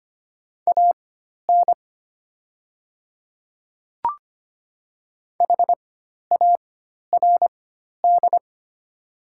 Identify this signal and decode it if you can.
Morse code, heard as a keyed tone, about 25 wpm, spaced as for 12 wpm, 700 Hz. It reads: AN HARD
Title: 4x Character Spacing - Top 300 Words - 25wpm